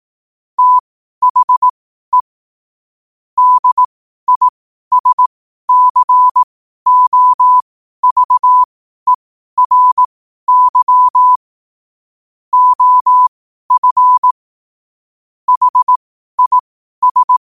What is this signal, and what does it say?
Morse code, heard as a keyed tone, about 18 wpm, spaced as for 12 wpm, 1000 Hz. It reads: THE DISCOVERY OF HIS
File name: qotdfiles / 12wpm